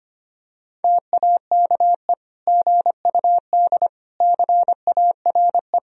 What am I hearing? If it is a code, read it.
Morse code, heard as a keyed tone, about 25 wpm, 700 Hz. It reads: TAKE GUD CARE
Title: Sentences from Top 400 Words and Abbreviations 25wpm